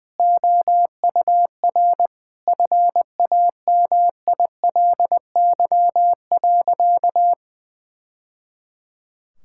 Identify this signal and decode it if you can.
Morse code, heard as a keyed tone, about 21 wpm, 700 Hz. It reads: OUR FAMILY.
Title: Sentences from Top 1000 Words - Review Part 6 - 20wpm